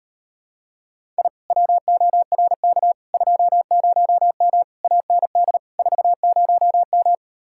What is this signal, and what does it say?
Morse code, heard as a keyed tone, about 38 wpm, 700 Hz. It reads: I WORK 20M AND 40M